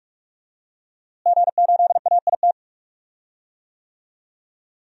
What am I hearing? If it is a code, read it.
Morse code, heard as a keyed tone, about 46 wpm, 700 Hz. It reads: G8AIT
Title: DXCC-100 Call Signs - 5 Characters - Repeated 3x - 45wpm